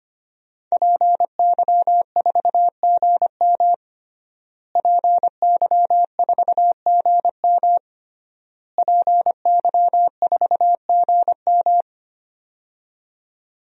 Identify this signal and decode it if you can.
Morse code, heard as a keyed tone, about 25 wpm, 700 Hz. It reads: PY4GM PY4GM PY4GM
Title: DXCC-100 Call Signs - 5 Characters - Repeated 3x - 25wpm